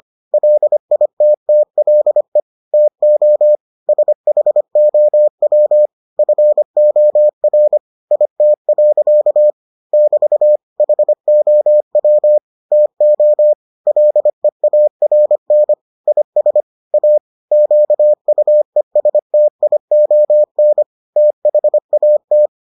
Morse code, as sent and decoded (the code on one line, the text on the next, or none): .-.. .. - - .-.. / - --- / ... .... --- .-- / ..-. --- .-. / .. - .-.-.- / -...- / .... --- .-- / - --- / .-.. .- .-. -. / .. ... / .- / --.- ..- . ... - .. --- -. / - .... .- -
LITTLE TO SHOW FOR IT. <BT> HOW TO LEARN IS A QUESTION THAT